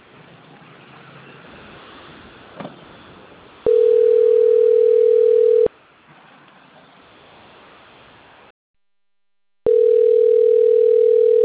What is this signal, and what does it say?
Ringback tone (North America), 440 + 480 Hz.
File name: rg-1600-unknown- - - .19120